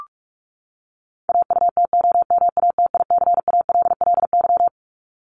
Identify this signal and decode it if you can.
Morse code, heard as a keyed tone, about 44 wpm, 700 Hz. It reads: AUTOMATICALLY
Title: Top 1751-2000 Words 45wpm